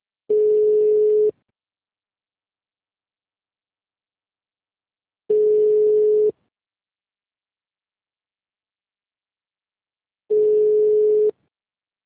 Ringback tone (Europe, 425 Hz).
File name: out- -2007- - - .17